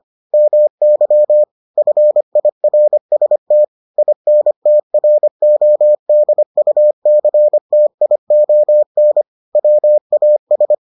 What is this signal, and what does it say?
Morse code, heard as a keyed tone, about 25 wpm, 600 Hz. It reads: MY FIRST INTRODUCTION WAS